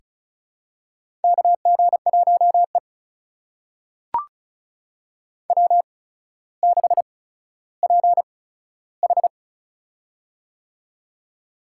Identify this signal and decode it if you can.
Morse code, heard as a keyed tone, about 36 wpm, spaced as for 15 wpm, 700 Hz. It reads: KG1E W6PH
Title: USA Call Signs - 4 Characters - 8x Character Spacing - 35wpm